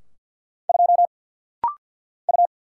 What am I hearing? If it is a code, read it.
Morse code, heard as a keyed tone, about 50 wpm, 700 Hz. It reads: 2 U
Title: Single Letter-Number Alt - Rapid-Fire 50wpm